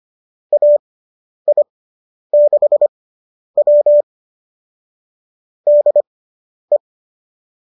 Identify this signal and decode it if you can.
Morse code, heard as a keyed tone, about 25 wpm, spaced as for 10 wpm, 590 Hz. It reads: AI6W DE